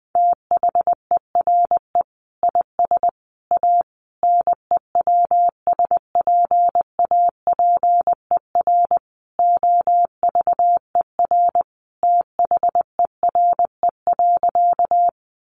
Morse code, heard as a keyed tone, about 20 wpm, 700 Hz. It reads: THERE IS A NEWSPAPER OVER THERE.
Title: Sentences from Top 1100 Words 20wpm